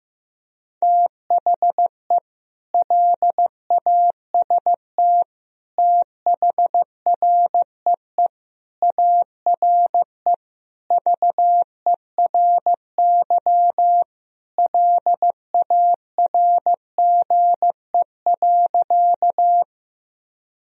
Morse code, heard as a keyed tone, about 16 wpm, 700 Hz. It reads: THE LAST THREE ARE VERY LARGE.
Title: Sentences from Top 1000 Words - Review Part 1 - 15wpm